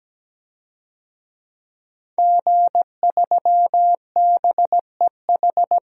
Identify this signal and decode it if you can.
Morse code, heard as a keyed tone, about 17 wpm, 700 Hz. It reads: G3BEH
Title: DXCC-100 Call Signs - 5 Characters - 17wpm